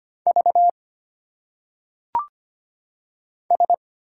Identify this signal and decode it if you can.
Morse code, heard as a keyed tone, about 25 wpm, 700 Hz. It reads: V S
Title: Single Letters - Phonetic Pronunciation 25wpm